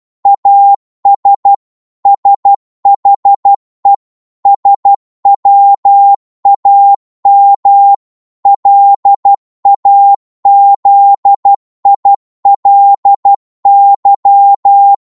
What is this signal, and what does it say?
Morse code, heard as a keyed tone, about 12 wpm, 800 Hz. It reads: AS SHE SWAM LAZILY